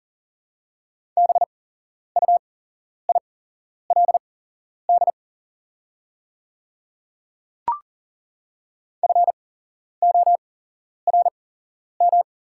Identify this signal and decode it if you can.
Morse code, heard as a keyed tone, about 40 wpm, spaced as for 11 wpm, 700 Hz. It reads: BUILD FORM
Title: 8x Character Spacing - Top 200 Words - 40wpm